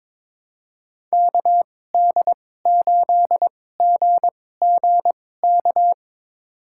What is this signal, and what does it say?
Morse code, heard as a keyed tone, about 22 wpm, spaced as for 16 wpm, 700 Hz. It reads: KD8GGK
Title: USA Call Signs - 6 Characters - 2x Character Spacing - 22wpm